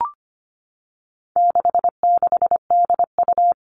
Morse code, heard as a keyed tone, about 25 wpm, 700 Hz. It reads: 66DU
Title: All Cut Numbers - 4-Digit 25wpm